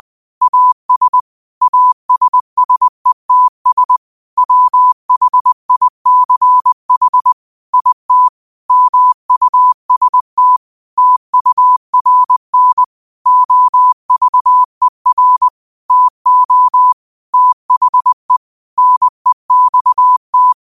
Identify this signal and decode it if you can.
Morse code, heard as a keyed tone, about 20 wpm, 1000 Hz. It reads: AS ASSETS WHICH IT MUST TURN OVER TO THE NEXT
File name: qotdfiles / 20wpm